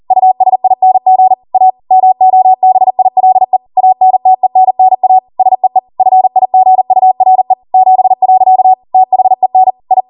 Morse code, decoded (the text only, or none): USING A MOBILE ANTENNA SEE FIGURE 7. THEN I